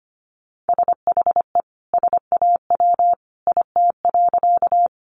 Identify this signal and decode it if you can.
Morse code, heard as a keyed tone, about 25 wpm, 700 Hz. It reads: SHE SAW IT.
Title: Sentences from Top 300 Words 25wpm